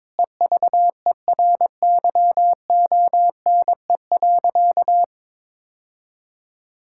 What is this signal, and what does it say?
Morse code, heard as a keyed tone, about 22 wpm, 700 Hz. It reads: EVERYONE.
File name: practice / 22wpm / Sentences with 8-Word Phrases - 22wpm